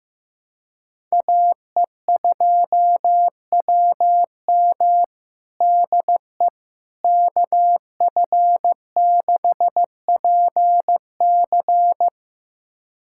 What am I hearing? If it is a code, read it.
Morse code, heard as a keyed tone, about 16 wpm, 700 Hz. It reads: AE2WM DE KF6PC
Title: USA Callsign DE Callsign - 5 Characters - 15wpm